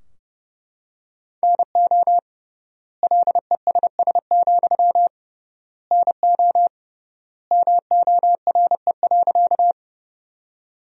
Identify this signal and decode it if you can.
Morse code, heard as a keyed tone, about 29 wpm, 700 Hz. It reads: NO LESS, NO MORE.
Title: 3x Word-Spacing - Sentences from Top 400 Words - 30wpm